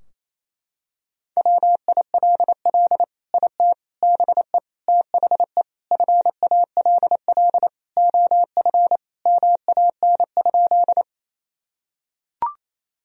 Morse code, heard as a keyed tone, about 28 wpm, 700 Hz. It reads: WILL IT BE THE FALL OF MAN?
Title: Sentences from Top 1000 Words - Review Part 2 - 28wpm